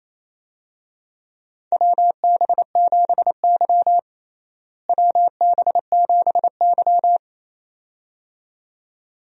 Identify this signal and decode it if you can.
Morse code, heard as a keyed tone, about 28 wpm, 700 Hz. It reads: WB7Y WB7Y
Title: USA Call Signs - 4 Characters - Repeated 2x - 28wpm